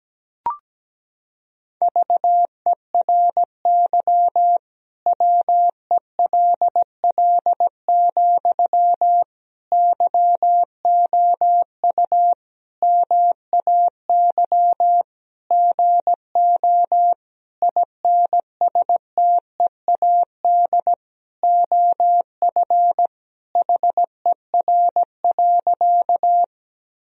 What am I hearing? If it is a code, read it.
Morse code, heard as a keyed tone, about 17 wpm, 700 Hz. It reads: VERY WELL, YOU MAY GO INSTEAD OF HER.
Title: Sentences from Top 1000 Words - Review Part 4 - 17wpm